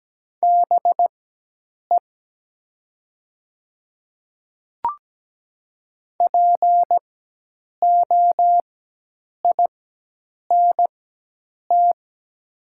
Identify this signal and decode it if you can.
Morse code, heard as a keyed tone, about 17 wpm, spaced as for 8 wpm, 700 Hz. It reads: BE POINT
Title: 4x Character Spacing - Top 200 Words - 17wpm